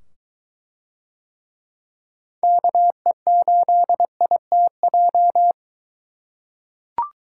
Morse code, heard as a keyed tone, about 23 wpm, 700 Hz. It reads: KE8ITJ